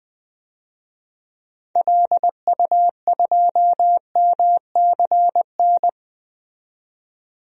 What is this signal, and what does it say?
Morse code, heard as a keyed tone, about 20 wpm, 700 Hz. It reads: LU2MCN
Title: DXCC-100 Call Signs - 6 Characters - 20wpm